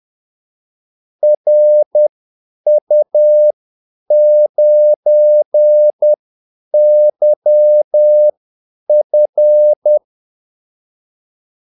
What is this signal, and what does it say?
Morse code, heard as a keyed tone, about 10 wpm, spaced as for 8 wpm, 600 Hz. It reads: RU9YF
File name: Callsigns-10-8wpm